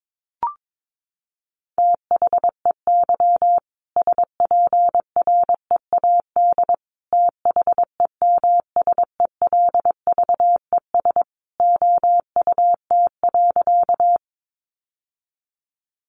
Morse code, heard as a keyed tone, about 22 wpm, 700 Hz. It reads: THEY SPREAD THEMSELVES OUT.
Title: Sentences from Top 1200 Words 22wpm